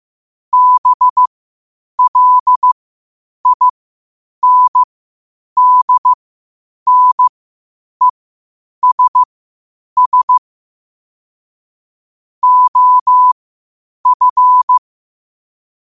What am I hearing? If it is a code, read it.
Morse code, heard as a keyed tone, about 15 wpm, spaced as for 8 wpm, 1000 Hz. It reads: BLINDNESS OF
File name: qotdfiles / 8wpm